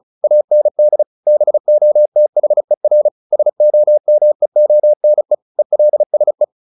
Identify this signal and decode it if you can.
Morse code, heard as a keyed tone, about 34 wpm, 600 Hz. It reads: AND BOTHER SOMEONE ELSE
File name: SS309 35